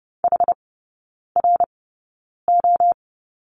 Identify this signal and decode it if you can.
Morse code, heard as a keyed tone, about 30 wpm, spaced as for 9 wpm, 700 Hz. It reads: HRO
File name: Lesson 19e - Letter w - Three Characters 30wpm